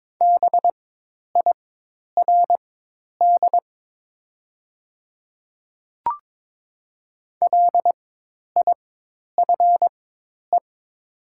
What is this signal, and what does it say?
Morse code, heard as a keyed tone, about 22 wpm, spaced as for 10 wpm, 700 Hz. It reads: BIRD LIFE
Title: 4x Character Spacing - Top 500 Words - 22wpm